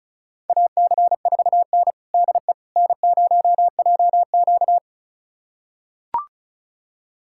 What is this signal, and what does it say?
Morse code, heard as a keyed tone, about 36 wpm, 700 Hz. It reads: AC4N DE N0JQ
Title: USA Callsign DE Callsign - 4 Characters - 35wpm